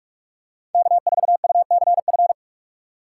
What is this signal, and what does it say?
Morse code, heard as a keyed tone, about 44 wpm, 700 Hz. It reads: K4UCF